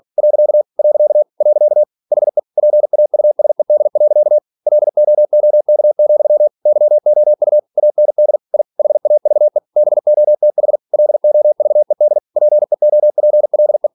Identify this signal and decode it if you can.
Morse code, heard as a keyed tone, about 46 wpm, 600 Hz. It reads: . . HE PAUSED. LOOK, YOU AND I HAVE BOTH LOVED PEOPLE
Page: Never 47wpm